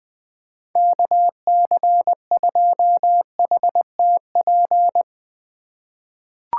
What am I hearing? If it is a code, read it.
Morse code, heard as a keyed tone, about 20 wpm, 700 Hz. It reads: KC2HTP
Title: USA Call Signs - 6 Characters - 4x Character Spacing - 20wpm